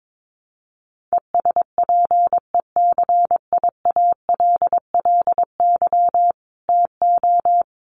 Morse code, heard as a keyed tone, about 22 wpm, 700 Hz. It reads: ESPECIALLY TO
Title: Sentences from Top 1500 Words 22wpm